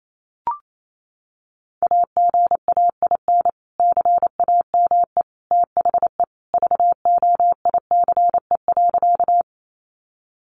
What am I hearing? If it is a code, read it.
Morse code, heard as a keyed tone, about 28 wpm, 700 Hz. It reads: AGAIN CAME THE VOICE.
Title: Sentences from Top 500 Words 28wpm